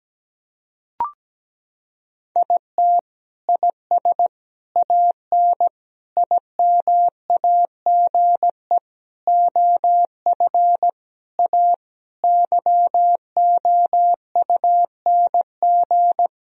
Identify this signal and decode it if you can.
Morse code, heard as a keyed tone, about 17 wpm, 700 Hz. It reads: IT IS AN IMAGE OF A YOUNG
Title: Sentences from Top 1000 Words - Review Part 4 - 17wpm